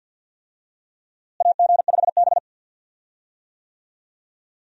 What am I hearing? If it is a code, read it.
Morse code, heard as a keyed tone, about 47 wpm, 700 Hz. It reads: AG5B
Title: USA Call Signs - 4 Characters - Repeated 2x - 50wpm